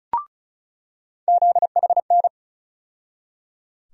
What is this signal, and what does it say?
Morse code, heard as a keyed tone, about 36 wpm, 700 Hz. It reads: ZHN
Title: USA Call Sign Postfixes - 3 Letters 35wpm